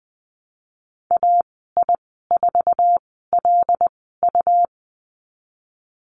Morse code, heard as a keyed tone, about 20 wpm, spaced as for 15 wpm, 700 Hz. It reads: AI4LU
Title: 186 Lesson 30g - Letter b - Call Signs 20wpm